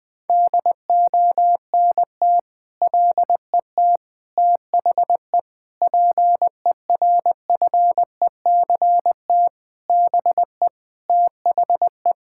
Morse code, heard as a keyed tone, about 20 wpm, 700 Hz. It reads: DONT LET THE PERFECT BE THE
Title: Sentences with 9-Word Phrases - 20wpm